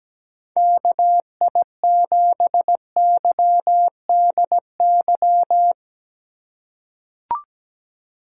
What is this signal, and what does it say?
Morse code, heard as a keyed tone, about 17 wpm, 700 Hz. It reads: KI7YDY
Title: USA Call Signs - 6 Characters - 4x Character Spacing - 17wpm